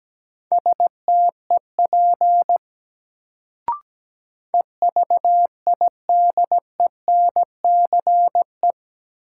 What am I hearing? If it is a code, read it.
Morse code, heard as a keyed tone, about 17 wpm, 700 Hz. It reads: STEP EVIDENCE
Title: Sets of 5 Words Encore - New Word Drill - 17wpm